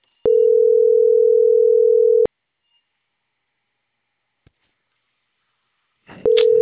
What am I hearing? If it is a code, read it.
Ringback tone (North America), 440 + 480 Hz.